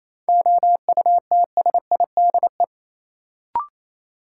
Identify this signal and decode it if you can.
Morse code, heard as a keyed tone, about 28 wpm, 700 Hz. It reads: OUTSIDE
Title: Sentences from Top 900 Words - New Word Drill - 28wpm